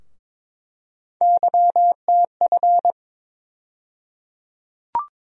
Morse code, heard as a keyed tone, about 22 wpm, 700 Hz. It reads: YTF
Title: USA Call Sign Postfixes - 3 Letters 22wpm